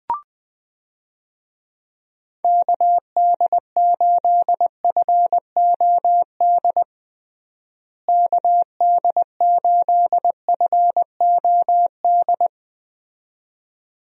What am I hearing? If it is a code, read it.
Morse code, heard as a keyed tone, about 21 wpm, 700 Hz. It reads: KD8FOD KD8FOD
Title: USA Call Signs - 6 Characters - Repeated 2x - 20wpm